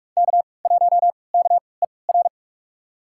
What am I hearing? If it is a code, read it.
Morse code, heard as a keyed tone, about 46 wpm, spaced as for 25 wpm, 700 Hz. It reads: K1KER
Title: USA Call Signs - 5 Characters - 3x Character Spacing - 45wpm